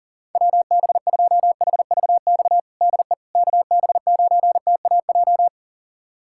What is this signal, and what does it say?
Morse code, heard as a keyed tone, about 40 wpm, 700 Hz. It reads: WB2HVX DE KB9TAJ
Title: USA Callsign DE Callsign - 6 Characters - 40wpm